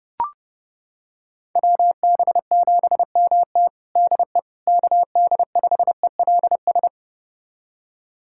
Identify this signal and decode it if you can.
Morse code, heard as a keyed tone, about 32 wpm, 700 Hz. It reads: WB7MT DE KD5ELS
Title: USA Callsign DE Callsign 30wpm